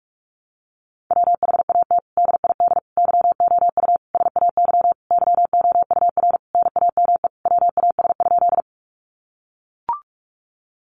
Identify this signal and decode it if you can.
Morse code, heard as a keyed tone, about 44 wpm, 700 Hz. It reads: WHAT DID YOU SAY YOUR NAME WAS?
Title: Sentences from Top 200 Words 45wpm